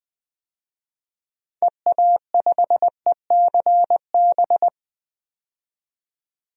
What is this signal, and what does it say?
Morse code, heard as a keyed tone, about 20 wpm, 700 Hz. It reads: EA5ECB
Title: DXCC-100 Call Signs - 6 Characters - 20wpm